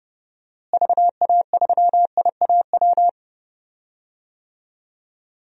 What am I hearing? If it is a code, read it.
Morse code, heard as a keyed tone, about 32 wpm, 700 Hz. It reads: VA3IAW
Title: DXCC-100 Call Signs - 6 Characters - Repeated 3x - 30wpm